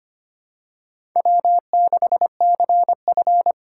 Morse code, heard as a keyed tone, about 25 wpm, 700 Hz. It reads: W6CF
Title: USA Call Signs - 4 Characters - 4x Character Spacing - 25wpm